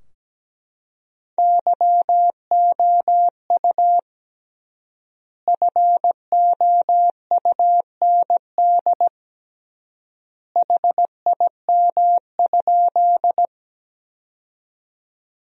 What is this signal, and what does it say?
Morse code, heard as a keyed tone, about 17 wpm, 700 Hz. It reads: YOU FOUND HIM?
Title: 3x Word-Spacing - Sentences from Top 300 Words - 17wpm